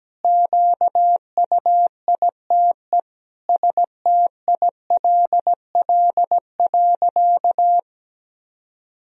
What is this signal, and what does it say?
Morse code, heard as a keyed tone, about 17 wpm, 700 Hz. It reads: QUITE STILL.
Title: Sentences from Top 1000 Words - Review Part 2 - 17wpm